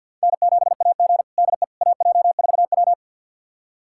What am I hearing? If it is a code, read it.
Morse code, heard as a keyed tone, about 51 wpm, 700 Hz. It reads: N7AG DE AJ4W
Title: USA Callsign DE Callsign - 4 Characters - 50wpm